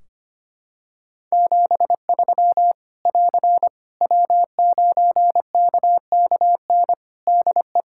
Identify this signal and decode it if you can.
Morse code, heard as a keyed tone, about 25 wpm, 700 Hz. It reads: 73 <AR> W9KKN DE
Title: Practice QSO - Exchange 4 - Ending - 25wpm